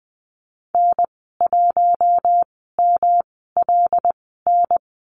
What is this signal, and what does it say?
Morse code, heard as a keyed tone, about 20 wpm, spaced as for 14 wpm, 700 Hz. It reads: N1MLN
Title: 128 Lesson 21g - Number 6 - Call Signs 20wpm